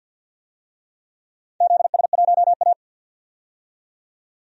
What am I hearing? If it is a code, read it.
Morse code, heard as a keyed tone, about 53 wpm, 700 Hz. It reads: ZS1A
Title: DXCC-100 Call Signs - 4 Characters - 50wpm